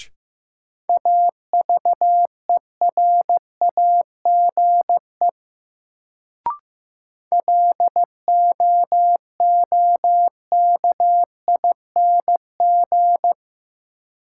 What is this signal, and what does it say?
Morse code, heard as a keyed tone, about 15 wpm, 700 Hz. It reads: AVERAGE LOOKING
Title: Sentences from Top 1000 Words - Review Part 6 - New Word Drill - 15wpm